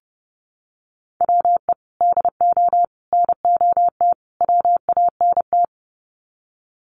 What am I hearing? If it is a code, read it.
Morse code, heard as a keyed tone, about 30 wpm, 700 Hz. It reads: WE DO NOT WANT
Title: Sets of 4 Words 30wpm